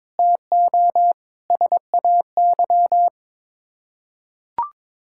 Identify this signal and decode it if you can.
Morse code, heard as a keyed tone, about 21 wpm, 700 Hz. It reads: TO SAY
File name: practice / 22wpm / Sets of 8 Words - 22wpm